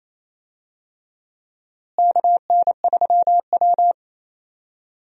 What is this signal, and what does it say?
Morse code, heard as a keyed tone, about 28 wpm, 700 Hz. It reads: KN3W
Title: USA Call Signs - 4 Characters - Repeated 2x - 28wpm